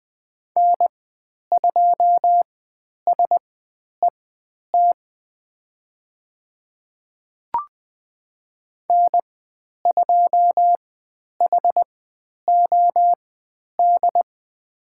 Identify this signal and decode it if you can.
Morse code, heard as a keyed tone, about 20 wpm, spaced as for 10 wpm, 700 Hz. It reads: N2SET N2HOD
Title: 102 Lesson 17g - Letter u - Call Signs 20wpm